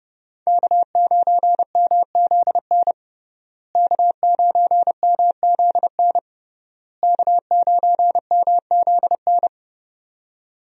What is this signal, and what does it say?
Morse code, heard as a keyed tone, about 30 wpm, 700 Hz. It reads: K9MZN K9MZN K9MZN
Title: USA Call Signs - 5 Characters - Repeated 3x - 30wpm